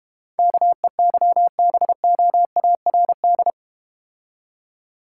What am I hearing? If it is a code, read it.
Morse code, heard as a keyed tone, about 32 wpm, 700 Hz. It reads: KEYBOARD